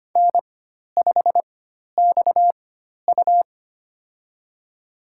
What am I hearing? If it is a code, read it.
Morse code, heard as a keyed tone, about 25 wpm, spaced as for 12 wpm, 700 Hz. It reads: N5XU
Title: USA Call Signs - 4 Characters - 4x Character Spacing - 25wpm